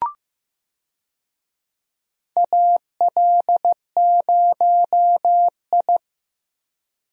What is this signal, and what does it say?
Morse code, heard as a keyed tone, about 15 wpm, 700 Hz. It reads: AL0I
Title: USA Call Signs - 4 Characters - 15wpm